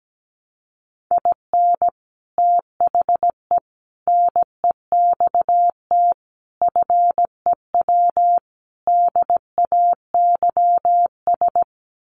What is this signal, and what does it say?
Morse code, heard as a keyed tone, about 17 wpm, 700 Hz. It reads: IN THE NEXT FEW DAYS
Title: Sets of 5 Words Encore2 17wpm